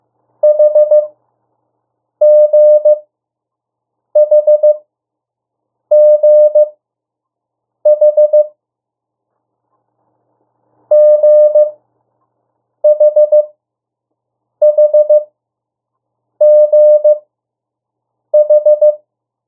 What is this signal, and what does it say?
Morse code, heard as a keyed tone, about 15 wpm, spaced as for 6 wpm, 600 Hz. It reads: HGHGH GHHGH
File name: morse mp3 / 02 urok 1